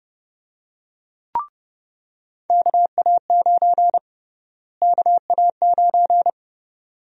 Morse code, heard as a keyed tone, about 30 wpm, 700 Hz. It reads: KA9 KA9
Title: USA Call Sign Prefixes - Repeated 2x - 30wpm